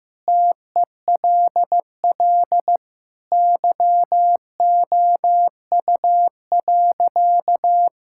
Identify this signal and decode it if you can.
Morse code, heard as a keyed tone, about 15 wpm, 700 Hz. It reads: TELL YOU.
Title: Sentences from Top 1000 Words - Review Part 5 - 15wpm